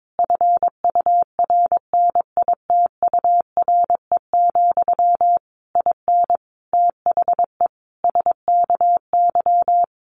Morse code, heard as a keyed tone, about 22 wpm, 700 Hz. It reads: FURNITURE, IN THE SKY
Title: A Sound Of Thunder 22wpm